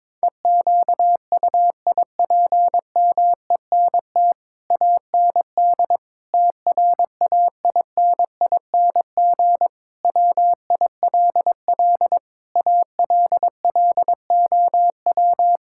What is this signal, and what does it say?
Morse code, heard as a keyed tone, about 22 wpm, 700 Hz. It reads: EQUIPMENT AND TRAINING WILL ALLOW
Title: Sentences from Top 1000 Words - Review Part 5 - 22wpm